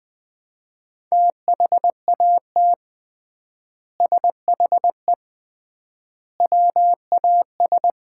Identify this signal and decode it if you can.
Morse code, heard as a keyed tone, about 21 wpm, 700 Hz. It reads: THAT SHE WAS
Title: 3x Word-Spacing - Sets of 3 Words - 20wpm